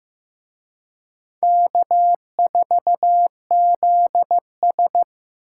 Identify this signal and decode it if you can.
Morse code, heard as a keyed tone, about 18 wpm, 700 Hz. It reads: K4ZS
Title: Speed Racing - USA 1x2 Call Signs 15wpm